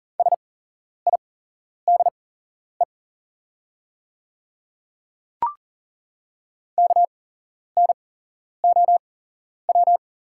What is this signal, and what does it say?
Morse code, heard as a keyed tone, about 40 wpm, spaced as for 11 wpm, 700 Hz. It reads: SIDE KNOW
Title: 8x Character Spacing - Top 300 Words - 40wpm